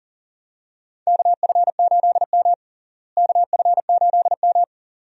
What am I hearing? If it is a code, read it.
Morse code, heard as a keyed tone, about 37 wpm, 700 Hz. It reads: KF8M KF8M
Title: USA Call Signs - 4 Characters - Repeated 2x - 40wpm